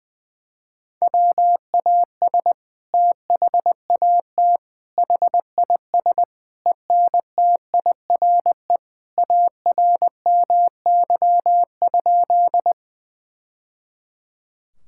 Morse code, heard as a keyed tone, about 20 wpm, 700 Hz. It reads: WAS THAT HIS ENTIRE ARMY?
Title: Sentences from Top 1000 Words - Review Part 5 - 20wpm